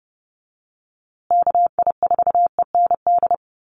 Morse code, heard as a keyed tone, about 30 wpm, 700 Hz. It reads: KI4END
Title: USA Call Signs Repeated 3x 30wpm